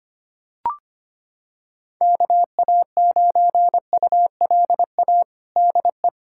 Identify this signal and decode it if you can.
Morse code, heard as a keyed tone, about 25 wpm, 700 Hz. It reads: KA9ULA DE
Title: USA Callsign DE Callsign - 6 Characters - 25wpm